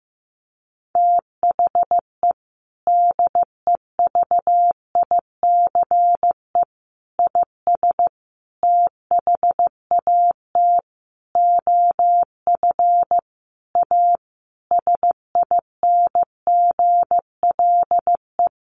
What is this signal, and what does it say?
Morse code, heard as a keyed tone, about 15 wpm, 700 Hz. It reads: THE DEVICE IS THAT OF A SINGLE
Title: Sentences from Top 1500 Words 15wpm